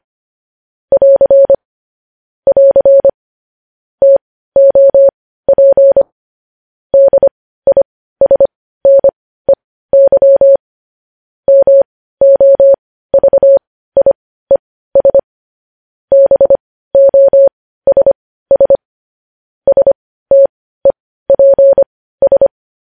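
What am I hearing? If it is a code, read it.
Morse code, heard as a keyed tone, about 25 wpm, spaced as for 15 wpm, 550 Hz. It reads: <AR> <AR> TOP DISNEY MOVIES BOSS STEPS
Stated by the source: Morse code news headlines sent at 15, preceded by vvv.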